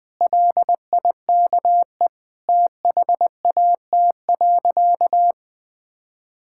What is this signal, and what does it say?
Morse code, heard as a keyed tone, about 20 wpm, 700 Hz. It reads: LIKE THAT.
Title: Sentences with 3-Word Phrases - 20wpm